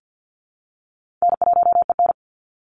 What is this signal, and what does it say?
Morse code, heard as a keyed tone, about 50 wpm, 700 Hz. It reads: N1EN
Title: Call Signs Repeated 3x - 50wpm